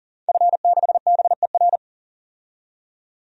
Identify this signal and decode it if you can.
Morse code, heard as a keyed tone, about 40 wpm, 700 Hz. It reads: F6BER
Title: DXCC-100 Call Signs - 5 Characters - 40wpm